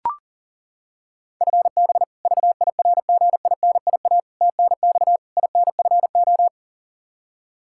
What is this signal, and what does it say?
Morse code, heard as a keyed tone, about 37 wpm, 700 Hz. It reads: FB VIRGINIA TNX INFO
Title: K1USN SST QSO Practice 40wpm